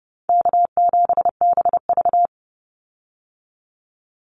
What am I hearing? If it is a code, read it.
Morse code, heard as a keyed tone, about 30 wpm, 700 Hz. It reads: K7BV
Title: Call Signs Repeated 3x - 30wpm